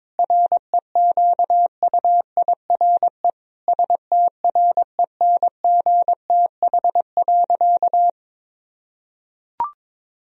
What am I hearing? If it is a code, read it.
Morse code, heard as a keyed tone, about 22 wpm, 700 Hz. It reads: REQUIRE STRENGTH.
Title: Sentences from Top 1000 Words - Review Part 4 - 22wpm